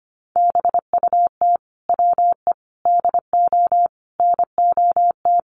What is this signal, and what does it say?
Morse code, heard as a keyed tone, about 25 wpm, 700 Hz. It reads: BUT WE DO NOT
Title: Sets of 5 Words Encore2 25wpm